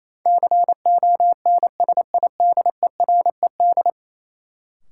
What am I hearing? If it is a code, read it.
Morse code, heard as a keyed tone, about 28 wpm, 700 Hz. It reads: CONSIDERED
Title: Sets of 5 Words - New Word Drill - 28wpm